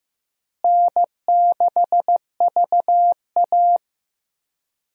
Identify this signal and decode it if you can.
Morse code, heard as a keyed tone, about 15 wpm, 700 Hz. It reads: N6VA
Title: USA Call Signs - 4 Characters - Repeated 3x - 15wpm